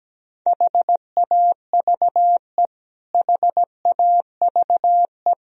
Morse code, heard as a keyed tone, about 17 wpm, 700 Hz. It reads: HAVE HAVE
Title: Top 100 Words Repeated 17wpm